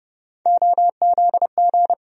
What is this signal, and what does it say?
Morse code, heard as a keyed tone, about 30 wpm, 700 Hz. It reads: OZG
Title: USA Call Sign Postfixes - 3 Letters 30wpm